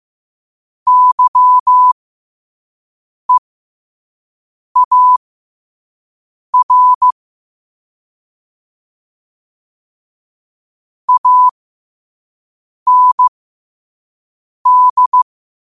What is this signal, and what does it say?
Morse code, heard as a keyed tone, about 15 wpm, spaced as for 5 wpm, 1000 Hz. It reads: YEAR AND
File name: qotdfiles / 5wpm